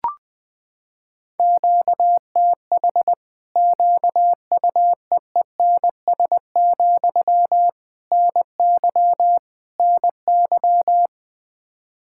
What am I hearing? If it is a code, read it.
Morse code, heard as a keyed tone, about 20 wpm, 700 Hz. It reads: QTH QUEENS, NY NY
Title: Practice QSO - Exchange 1 - Beginning - 20wpm